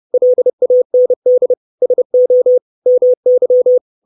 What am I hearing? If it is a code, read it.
Morse code, heard as a keyed tone, about 30 wpm, 500 Hz. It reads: LAND SO MY
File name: 632 Words 30WPM